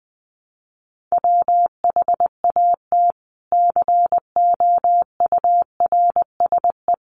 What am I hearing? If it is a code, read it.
Morse code, heard as a keyed tone, about 19 wpm, 700 Hz. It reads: WHAT COURSE
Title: Sentences from Top 1200 Words 20wpm